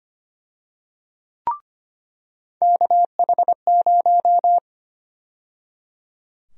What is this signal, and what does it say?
Morse code, heard as a keyed tone, about 25 wpm, 700 Hz. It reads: KH0
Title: USA Call Sign Prefixes - 25wpm